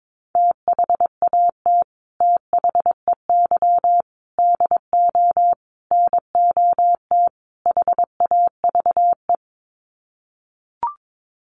Morse code, heard as a keyed tone, about 22 wpm, 700 Hz. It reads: THAT THEY DO NOT HAVE
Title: Sets of 5 Words Encore2 22wpm